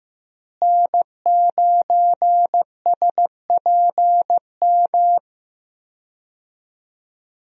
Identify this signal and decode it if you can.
Morse code, heard as a keyed tone, about 15 wpm, 700 Hz. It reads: N9SPM
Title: USA Call Signs - 5 Characters - Repeated 2x - 15wpm